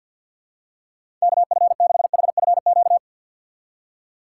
Morse code, heard as a keyed tone, about 52 wpm, 700 Hz. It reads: KF6HFQ
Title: USA Call Signs - 3x Character Spacing - 50wpm